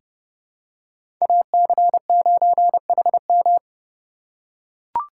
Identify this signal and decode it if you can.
Morse code, heard as a keyed tone, about 30 wpm, 700 Hz. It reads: AC9HM
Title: USA Call Signs - 5 Characters - 3x Character Spacing - 30wpm